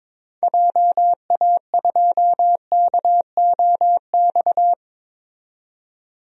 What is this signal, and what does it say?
Morse code, heard as a keyed tone, about 22 wpm, 700 Hz. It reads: JA2KOX
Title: DXCC-100 Call Signs - 6 Characters - 22wpm